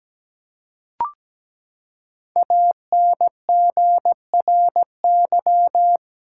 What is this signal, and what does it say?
Morse code, heard as a keyed tone, about 17 wpm, 700 Hz. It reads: ANGRY
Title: Five Letter Words - 17wpm